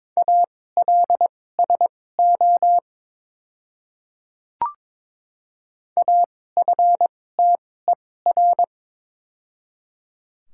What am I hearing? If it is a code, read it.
Morse code, heard as a keyed tone, about 22 wpm, spaced as for 16 wpm, 700 Hz. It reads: ALSO AFTER
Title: 2x Character Spacing - Top 200 Words - 22wpm